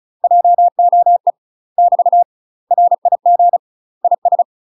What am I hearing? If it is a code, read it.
Morse code, heard as a keyed tone, about 35 wpm, 700 Hz. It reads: JOE <BT> RIG IS
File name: UK MASTER QSOs TRAINING / 35WPM / UK MASTER QSOs 0033